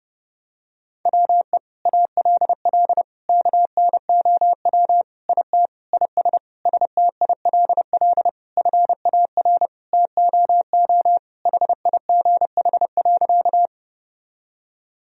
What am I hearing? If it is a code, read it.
Morse code, heard as a keyed tone, about 30 wpm, 700 Hz. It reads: WE ALL KNOW IT IS STILL FAR TOO HIGH.
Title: Sentences from Top 1000 Words - Review Part 1 - 30wpm